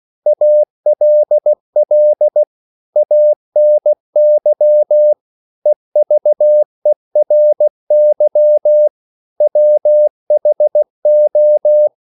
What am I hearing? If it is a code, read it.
Morse code, heard as a keyed tone, about 16 wpm, 600 Hz. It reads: ALL ANY EVERY WHO
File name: CW / Practice / 500MostCommonEnglishWords16